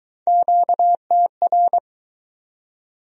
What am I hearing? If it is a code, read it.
Morse code, heard as a keyed tone, about 20 wpm, 700 Hz. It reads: QTR